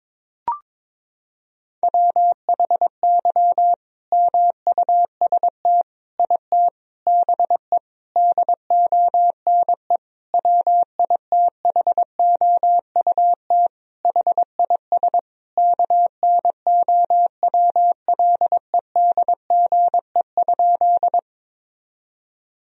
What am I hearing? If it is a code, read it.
Morse code, heard as a keyed tone, about 22 wpm, 700 Hz. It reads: WHY MUST IT BE DONE WITHOUT HIS KNOWLEDGE?
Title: Sentences from Top 1000 Words - Review Part 3 - 22wpm